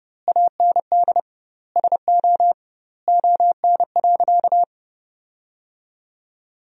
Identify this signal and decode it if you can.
Morse code, heard as a keyed tone, about 30 wpm, 700 Hz. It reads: AND SO ON.
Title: 2x Word-Spacing - Sentences from Top 100 Words - 30wpm